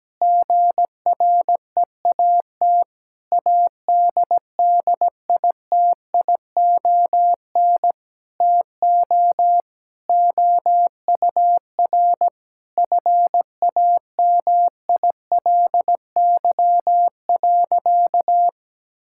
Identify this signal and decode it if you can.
Morse code, heard as a keyed tone, about 17 wpm, 700 Hz. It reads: GREAT ADDITION TO OUR FAMILY.
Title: Sentences from Top 1000 Words - Review Part 6 - 17wpm